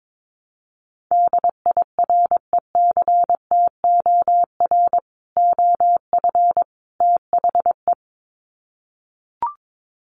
Morse code, heard as a keyed tone, about 22 wpm, 700 Hz. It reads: DIRECTOR OF THE
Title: Sets of 3 Words 22wpm